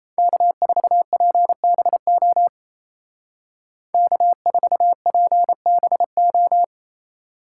Morse code, heard as a keyed tone, about 33 wpm, 700 Hz. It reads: K4PBO K4PBO
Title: Speed Racing - USA 1x3 Call Signs 28wpm